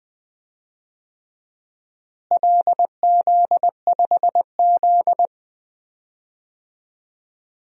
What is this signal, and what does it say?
Morse code, heard as a keyed tone, about 21 wpm, 700 Hz. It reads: LZ5Z
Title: DXCC-100 Call Signs - 4 Characters - Repeated 3x - 20wpm